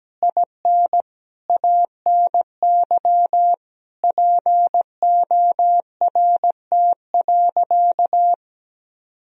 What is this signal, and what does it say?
Morse code, heard as a keyed tone, about 17 wpm, 700 Hz. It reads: IN ANY PORT.
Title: Sentences from Top 1000 Words - Review Part 2 - 17wpm